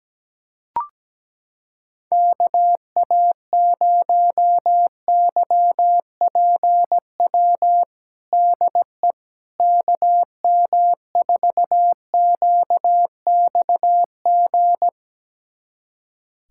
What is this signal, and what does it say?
Morse code, heard as a keyed tone, about 17 wpm, 700 Hz. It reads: KA0YPW DE KM4QXG
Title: USA Callsign DE Callsign - 6 Characters - 17wpm